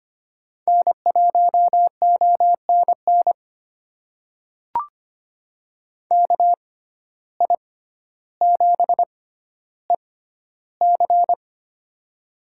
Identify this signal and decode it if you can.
Morse code, heard as a keyed tone, about 25 wpm, spaced as for 13 wpm, 700 Hz. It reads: N1ONN KI7EC
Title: USA Call Signs - 5 Characters - 6x Character Spacing - 25wpm